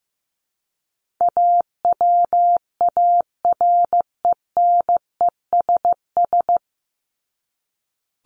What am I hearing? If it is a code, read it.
Morse code, heard as a keyed tone, about 15 wpm, 700 Hz. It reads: AWARENESS
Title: Top 1501-1750 Words 15wpm